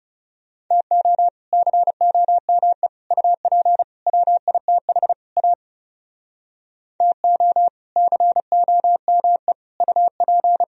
Morse code, heard as a keyed tone, about 40 wpm, 700 Hz. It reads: TO COME UP WITH A TO COME UP